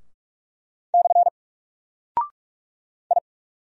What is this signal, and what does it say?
Morse code, heard as a keyed tone, about 46 wpm, 700 Hz. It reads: / I
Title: Single Character (Letter:Number:Punctuation:Prosign) Alt 45wpm